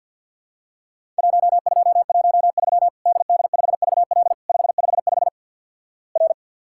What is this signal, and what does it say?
Morse code, heard as a keyed tone, about 50 wpm, 700 Hz. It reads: 1213 DD5VL 555 R
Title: Worked All Europe DX Contest - QTC Exchange with Repeats - 50wpm